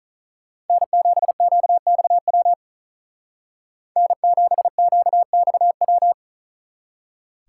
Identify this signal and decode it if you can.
Morse code, heard as a keyed tone, about 44 wpm, 700 Hz. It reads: N7QXW N7QXW
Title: Speed Racing - USA 1x3 Call Signs 35wpm